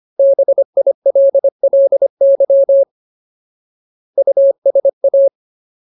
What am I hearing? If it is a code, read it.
Morse code, heard as a keyed tone, about 25 wpm, 550 Hz. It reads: BILLY USA
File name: Ultimate Word List / 25WPM Ultimate